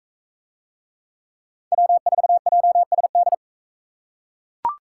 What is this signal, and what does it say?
Morse code, heard as a keyed tone, about 44 wpm, 700 Hz. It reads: W4JSD